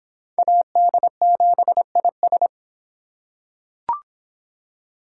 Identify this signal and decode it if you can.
Morse code, heard as a keyed tone, about 25 wpm, 700 Hz. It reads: AD7IS